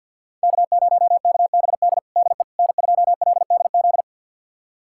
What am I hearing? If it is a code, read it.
Morse code, heard as a keyed tone, about 52 wpm, 700 Hz. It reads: K0KBD DE N2LDZ
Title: USA Callsign DE Callsign - 5 Characters - 50wpm